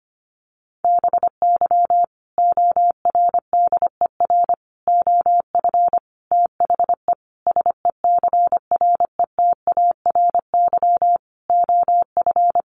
Morse code, heard as a keyed tone, about 25 wpm, 700 Hz. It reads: BY ORDER OF THE SECRETARY OF
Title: Sentences from Top 1300 Words 25wpm